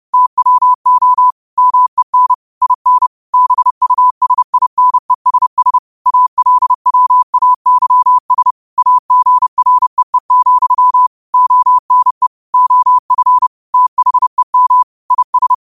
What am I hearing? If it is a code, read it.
Morse code, heard as a keyed tone, about 30 wpm, 1000 Hz. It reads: TWO MEN IN BUSINESS ALWAYS AGREE, ONE OF THEM IS
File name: qotdfiles / 30wpm